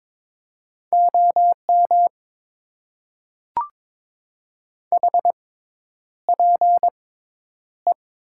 Morse code, heard as a keyed tone, about 21 wpm, spaced as for 10 wpm, 700 Hz. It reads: OM HPE
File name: Top 90 QSO Elements - 6x Character Spacing - 22wpm